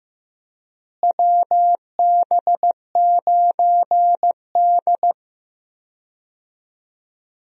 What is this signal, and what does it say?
Morse code, heard as a keyed tone, about 15 wpm, 700 Hz. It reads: WB9D
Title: USA Call Signs - 4 Characters - Repeated 2x - 15wpm